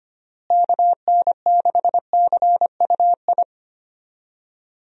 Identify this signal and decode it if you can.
Morse code, heard as a keyed tone, about 25 wpm, 700 Hz. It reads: KN6CUI
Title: USA Call Signs - 6 Characters - Repeated 3x - 25wpm